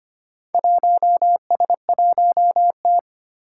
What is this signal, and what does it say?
Morse code, heard as a keyed tone, about 25 wpm, 700 Hz. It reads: 1S1T
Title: Four Challenging Letter-Number Combinations 25wpm